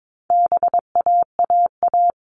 Morse code, heard as a keyed tone, about 22 wpm, 700 Hz. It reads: BAAA
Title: All Cut Numbers - 4-Digit 22wpm